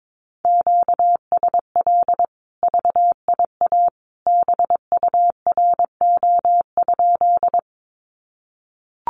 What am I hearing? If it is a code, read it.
Morse code, heard as a keyed tone, about 22 wpm, 700 Hz. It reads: QSL VIA BURO?
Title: Common QSO Phrases 22wpm